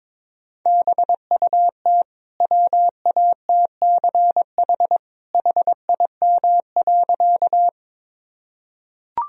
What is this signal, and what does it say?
Morse code, heard as a keyed tone, about 22 wpm, 700 Hz. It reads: BUT WATCH HIM.
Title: Sentences from Top 1000 Words - Review Part 1 - 22wpm